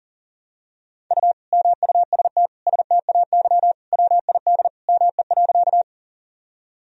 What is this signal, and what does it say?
Morse code, heard as a keyed tone, about 40 wpm, 700 Hz. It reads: U MUST STAY WID ME.
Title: Sentences from Top 500 Words and Abbreviations 40wpm